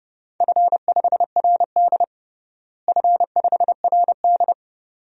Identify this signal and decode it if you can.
Morse code, heard as a keyed tone, about 30 wpm, 700 Hz. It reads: F5RD F5RD
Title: DXCC-100 Call Signs - 4 Characters - Repeated 2x - 30wpm